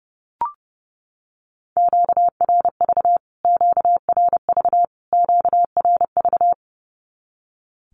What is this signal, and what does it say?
Morse code, heard as a keyed tone, about 30 wpm, 700 Hz. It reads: QRV QRV QRV
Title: Q-Codes Repeated 3x 30wpm